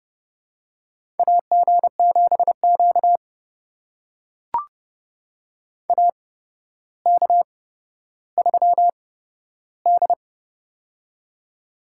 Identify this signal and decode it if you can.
Morse code, heard as a keyed tone, about 30 wpm, spaced as for 13 wpm, 700 Hz. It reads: AG7Q AK3D
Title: USA Call Signs - 4 Characters - 8x Character Spacing - 30wpm